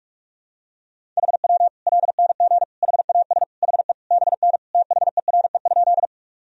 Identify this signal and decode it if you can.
Morse code, heard as a keyed tone, about 46 wpm, 700 Hz. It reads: HW LNG HAS HE BN THERE?
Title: Sentences from Top 100 Words and Abbreviations 45wpm